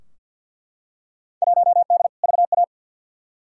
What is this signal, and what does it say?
Morse code, heard as a keyed tone, about 51 wpm, 700 Hz. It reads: 1D VA
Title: Field Day Exchanges - 50wpm